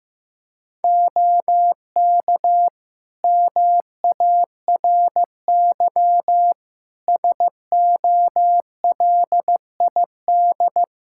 Morse code, heard as a keyed tone, about 15 wpm, 700 Hz. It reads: OK MARY SOLID
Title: Practice QSO - Exchange 3 - More Info - 15wpm